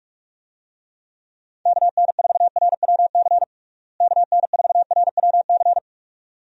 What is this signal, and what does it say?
Morse code, heard as a keyed tone, about 46 wpm, 700 Hz. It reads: KN4RWC KN4RWC
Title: USA Call Signs - 6 Characters - Repeated 2x - 45wpm